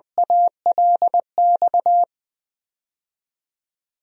Morse code, heard as a keyed tone, about 20 wpm, 700 Hz. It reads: ALX
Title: DXCC-100 Call Signs - 6 Characters - 20wpm